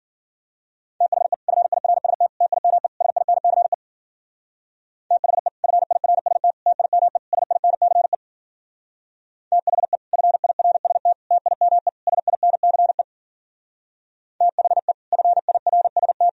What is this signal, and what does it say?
Morse code, heard as a keyed tone, about 52 wpm, 700 Hz. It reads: THE FIRST TIME SINCE THE FIRST TIME SINCE THE FIRST TIME SINCE THE FIRST